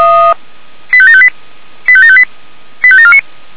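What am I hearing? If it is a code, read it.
Five-tone selective call (ZVEI), heard as quick runs of five tones, tones of about 70 ms. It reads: 85758 85758 85749
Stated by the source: ZVEI five-tone selective call 85729, received on an SDR.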